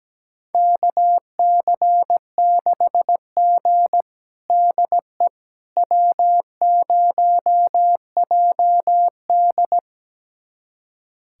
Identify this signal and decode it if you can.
Morse code, heard as a keyed tone, about 17 wpm, 700 Hz. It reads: KC6G DE W0JD
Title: USA Callsign DE Callsign - 4 Characters - 17wpm